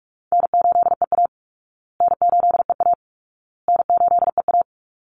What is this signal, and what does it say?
Morse code, heard as a keyed tone, about 44 wpm, 700 Hz. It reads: N8EA N8EA N8EA
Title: Call Signs Repeated 2x - 45wpm